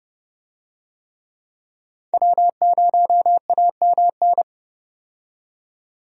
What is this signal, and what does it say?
Morse code, heard as a keyed tone, about 30 wpm, 700 Hz. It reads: W0AMN
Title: USA Call Signs - 5 Characters - 30wpm